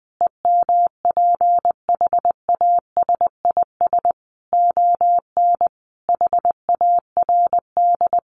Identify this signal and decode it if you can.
Morse code, heard as a keyed tone, about 20 wpm, 700 Hz. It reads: EMPHASIS ON HARD
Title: Sentences from Top 1400 Words 20wpm